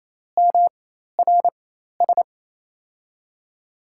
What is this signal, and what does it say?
Morse code, heard as a keyed tone, about 27 wpm, spaced as for 13 wpm, 700 Hz. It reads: MRS
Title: USA Call Signs - 4x Character Spacing - 28wpm